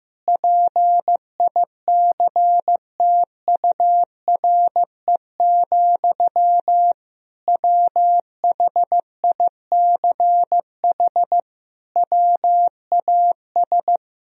Morse code, heard as a keyed tone, about 15 wpm, 700 Hz. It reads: PICTURE, WHICH WAS
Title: Sentences from Top 1000 Words - Review Part 3 - 15wpm